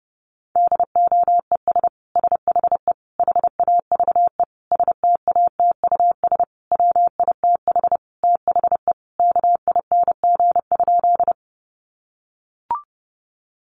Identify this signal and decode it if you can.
Morse code, heard as a keyed tone, about 30 wpm, 700 Hz. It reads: DOES SHE HAVE STATUS WITH THE KING?
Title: Sentences from Top 1400 Words 30wpm